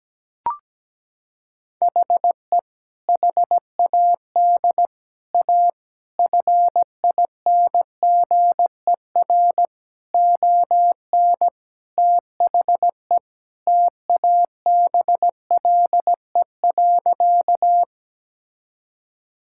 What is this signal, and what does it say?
Morse code, heard as a keyed tone, about 17 wpm, 700 Hz. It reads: HE HAD A FINGER ON THE TABLE.
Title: Sentences from Top 1600 Words 17wpm